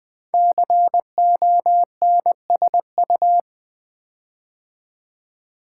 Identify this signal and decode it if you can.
Morse code, heard as a keyed tone, about 20 wpm, 700 Hz. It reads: CONSU
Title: Build A Word - 5 Letters At A Time - 12-Letter Words - 20wpm